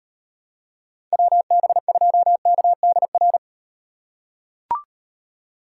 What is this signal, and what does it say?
Morse code, heard as a keyed tone, about 38 wpm, 700 Hz. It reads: WB2KDR